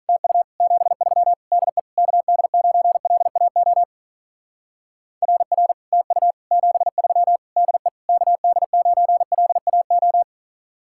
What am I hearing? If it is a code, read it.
Morse code, heard as a keyed tone, about 45 wpm, 700 Hz. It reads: TU 73 DE KD9LAO RR TU 73 DE KD9LAO